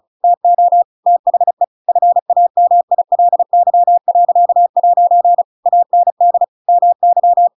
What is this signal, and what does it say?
Morse code, heard as a keyed tone, about 34 wpm, 700 Hz. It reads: TO THE FAMILY.' AND MY
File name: Pinocchio / Pinocchio ch7 35 35 ik1hge